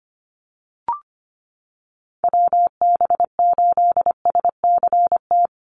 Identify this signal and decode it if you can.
Morse code, heard as a keyed tone, about 25 wpm, 700 Hz. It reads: WB8SCT
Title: Call Signs - 25wpm